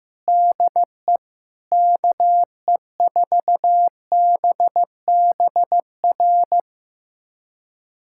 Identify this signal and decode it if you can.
Morse code, heard as a keyed tone, about 15 wpm, 700 Hz. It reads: DE KE4BBR
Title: USA Callsign DE Callsign - 6 Characters - 15wpm